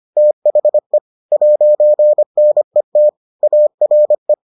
Morse code, heard as a keyed tone, about 25 wpm, 600 Hz. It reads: THE 'NET ARE